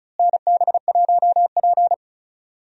Morse code, heard as a keyed tone, about 36 wpm, 700 Hz. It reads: NB1P
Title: USA Call Signs - 4 Characters - Repeated 3x - 35wpm